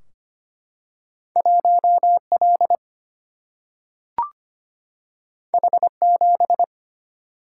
Morse code, heard as a keyed tone, about 25 wpm, 700 Hz. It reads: 1L H7
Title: Single-Letter Single-Number 25wpm